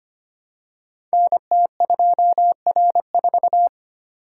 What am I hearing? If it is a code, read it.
Morse code, heard as a keyed tone, about 25 wpm, 700 Hz. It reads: NT2R4
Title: Code Groups - 5 Letter:Numbers - 25wpm